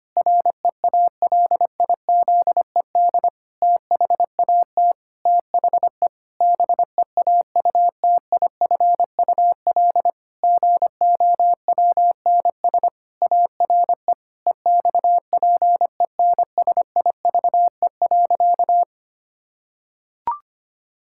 Morse code, heard as a keyed tone, about 25 wpm, 700 Hz. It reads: REALIZED THAT THE BEAUTIFUL GOWNS ARE EXPENSIVE.
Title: Pangram Challenge v2 - 25wpm